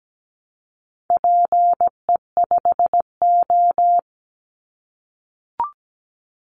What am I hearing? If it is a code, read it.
Morse code, heard as a keyed tone, about 17 wpm, 700 Hz. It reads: PE5O
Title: Call Signs - 17wpm